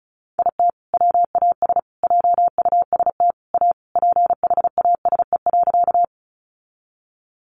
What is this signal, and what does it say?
Morse code, heard as a keyed tone, about 34 wpm, 700 Hz. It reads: IT WAS JUST A PHASE.
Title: Sentences from Top 1200 Words 35wpm